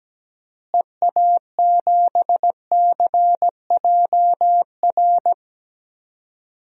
Morse code, heard as a keyed tone, about 17 wpm, 700 Hz. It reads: EA7CJR
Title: DXCC-100 Call Signs - 6 Characters - Repeated 3x - 17wpm